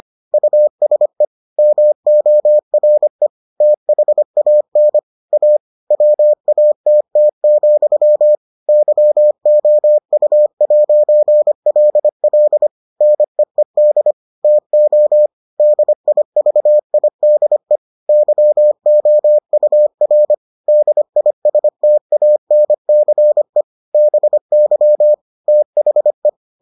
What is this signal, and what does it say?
Morse code, heard as a keyed tone, about 25 wpm, 600 Hz. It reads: USE MORE THAN A WATT, YOU'LL NEED TO DIVIDE YOUR DISTANCE BY THE